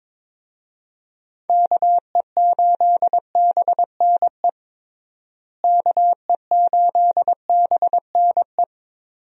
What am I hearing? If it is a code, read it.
Morse code, heard as a keyed tone, about 22 wpm, 700 Hz. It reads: KE8BNE KE8BNE
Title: USA Call Signs - 6 Characters - Repeated 2x - 22wpm